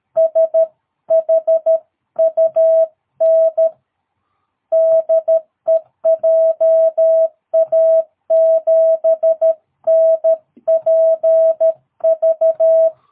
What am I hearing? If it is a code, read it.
Morse code, heard as a keyed tone, about 13 wpm, 660 Hz. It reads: SHUN DE JA7NPV
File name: jh8cbh / 2022cw